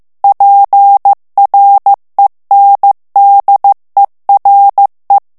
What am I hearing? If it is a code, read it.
Morse code, heard as a keyed tone, about 15 wpm, 800 Hz. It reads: PRENDERE
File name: morse mp3 / ITALIEN texte clair 2 vitesse 14 mots minute